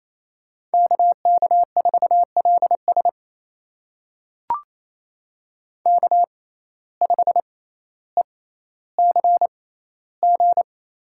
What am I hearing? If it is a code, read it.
Morse code, heard as a keyed tone, about 28 wpm, spaced as for 14 wpm, 700 Hz. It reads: KK4LS K5ECG
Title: USA Call Signs - 5 Characters - 6x Character Spacing - 28wpm